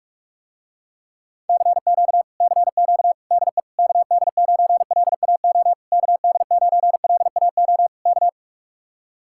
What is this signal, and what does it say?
Morse code, heard as a keyed tone, about 44 wpm, 700 Hz. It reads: CQ CQ DE KD9LAO KD9LAO K